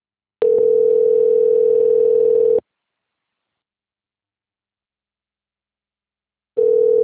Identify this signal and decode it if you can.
Ringback tone (North America), 440 + 480 Hz.
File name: rg-1600-unknown- - - .3822